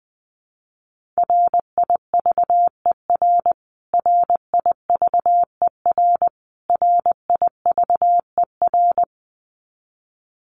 Morse code, heard as a keyed tone, about 19 wpm, 700 Hz. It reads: RIVER RIVER RIVER
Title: Top 300 Words Repeated 20wpm